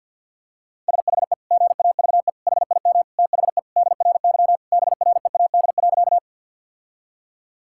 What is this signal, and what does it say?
Morse code, heard as a keyed tone, about 50 wpm, 700 Hz. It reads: SHE GAVE HIM THE DRY BREAD.
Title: Sentences from Top 1600 Words 50wpm